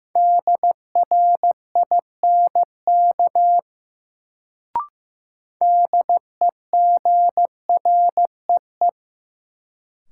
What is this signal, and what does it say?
Morse code, heard as a keyed tone, about 15 wpm, 700 Hz. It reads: DRINK DEGREE
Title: Sentences from Top 1500 Words - New Word Drill Alt - 15wpm